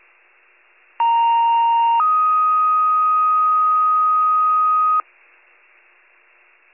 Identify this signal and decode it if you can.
Two-tone page, heard as a one-second tone, then a three-second tone, about 930 then 1250 Hz. It is heard from 1.0 to 5.0 s.